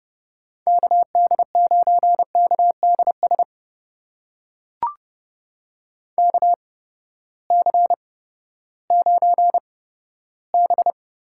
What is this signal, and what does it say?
Morse code, heard as a keyed tone, about 32 wpm, spaced as for 15 wpm, 700 Hz. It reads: KD9KDS KC9B
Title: USA Call Signs - 6 Characters - 8x Character Spacing - 30wpm